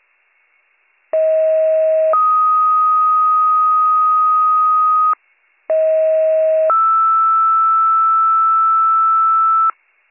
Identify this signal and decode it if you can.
Two-tone page, heard as a one-second tone, then a three-second tone, about 640 then 1250 Hz, about 640 then 1320 Hz. It is heard from 1.1 to 9.7 s.